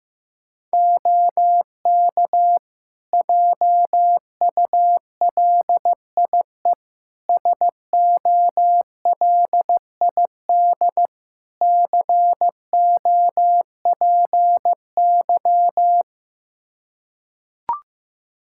Morse code, heard as a keyed tone, about 15 wpm, 700 Hz. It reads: OK JULIE SOLID COPY
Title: Practice QSO - Exchange 3 - More Info - 15wpm